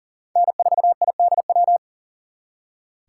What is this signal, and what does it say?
Morse code, heard as a keyed tone, about 50 wpm, 700 Hz. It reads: N4IDW